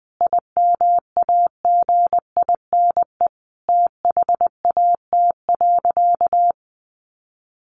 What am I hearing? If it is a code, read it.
Morse code, heard as a keyed tone, about 20 wpm, 700 Hz. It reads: IMAGINE THAT.
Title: Sentences from Top 1400 Words 20wpm